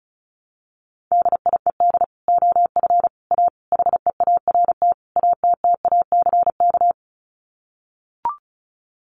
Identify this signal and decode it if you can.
Morse code, heard as a keyed tone, about 34 wpm, 700 Hz. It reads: DIED OF A HEART ATTACK
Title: Sets of 5 Words Encore2 35wpm